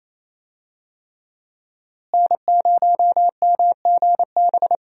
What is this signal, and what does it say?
Morse code, heard as a keyed tone, about 27 wpm, 700 Hz. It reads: N0MGB
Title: USA Call Signs - 5 Characters - Repeated 3x - 28wpm